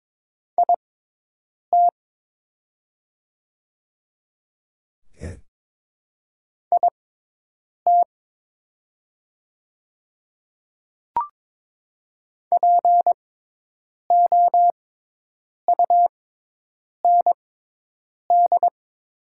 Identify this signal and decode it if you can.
Morse code, heard as a keyed tone, about 23 wpm, spaced as for 8 wpm, 700 Hz. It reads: IT IT POUND
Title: 6x Character Spacing - Top 500 Words - 22wpm